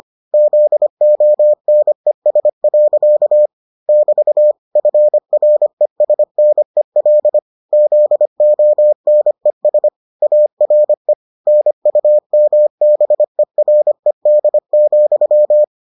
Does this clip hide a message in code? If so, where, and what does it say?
Yes: Morse code from 0.3 to 15.6 s, heard as a keyed tone, about 25 wpm, 600 Hz. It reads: ZONES. <BT> FRESNEL ZONES ARE NUMBERED,